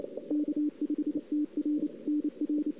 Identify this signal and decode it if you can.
Morse code, heard as a keyed tone, about 29 wpm, 300 Hz. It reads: K5TR NR
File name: CW / 2008-ss-cw-k5tr-op-n5rz -05:00